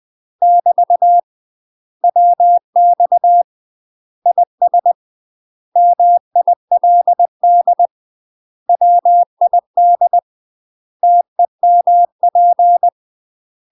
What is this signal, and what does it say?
Morse code, heard as a keyed tone, about 20 wpm, 700 Hz. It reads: <BT> WX IS MILD WID TEMP
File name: UK MASTER QSOs TRAINING / 20WPM / UK MASTER QSOs 0007